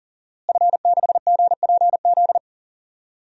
Morse code, heard as a keyed tone, about 38 wpm, 700 Hz. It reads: F6GPZ
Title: DXCC-100 Call Signs - 5 Characters - Repeated 3x - 40wpm